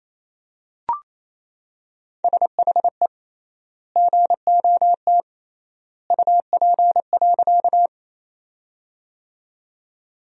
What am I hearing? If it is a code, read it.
Morse code, heard as a keyed tone, about 27 wpm, 700 Hz. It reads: SHE GOT UP.
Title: 3x Word-Spacing - Sentences from Top 300 Words - 28wpm